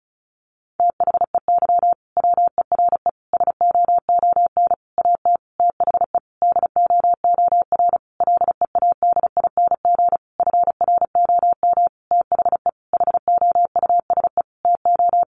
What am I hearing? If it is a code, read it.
Morse code, heard as a keyed tone, about 34 wpm, 700 Hz. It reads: THEY WERE SOON AT THE DOOR LEADING FROM THE HOUSE TO
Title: Sentences from Top 1200 Words 35wpm